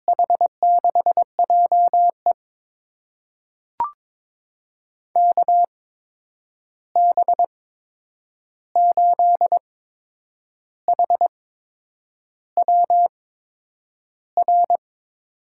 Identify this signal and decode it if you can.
Morse code, heard as a keyed tone, about 21 wpm, spaced as for 8 wpm, 700 Hz. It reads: H6JE KB8HWR
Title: USA Call Signs - 8x Character Spacing - 22wpm